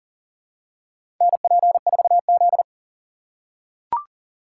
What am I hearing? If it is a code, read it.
Morse code, heard as a keyed tone, about 40 wpm, 700 Hz. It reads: NP4Z
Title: USA Call Signs - 4 Characters - 3x Character Spacing - 40wpm